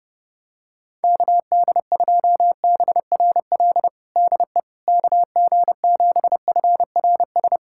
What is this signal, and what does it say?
Morse code, heard as a keyed tone, about 30 wpm, 700 Hz. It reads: KD2BRL DE KG7FRS
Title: USA Callsign DE Callsign 30wpm